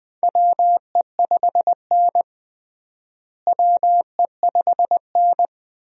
Morse code, heard as a keyed tone, about 21 wpm, 700 Hz. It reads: WE5N WE5N
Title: USA Call Signs - 4 Characters - Repeated 2x - 20wpm